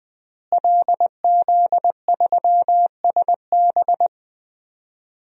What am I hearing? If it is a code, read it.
Morse code, heard as a keyed tone, about 20 wpm, 700 Hz. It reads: LZ3SB
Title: DXCC-100 Call Signs - 5 Characters - Repeated 3x - 20wpm